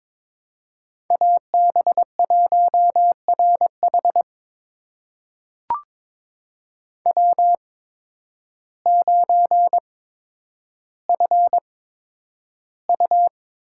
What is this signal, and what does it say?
Morse code, heard as a keyed tone, about 22 wpm, spaced as for 10 wpm, 700 Hz. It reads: AB1RH W9FU
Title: USA Call Signs - 5 Characters - 8x Character Spacing - 22wpm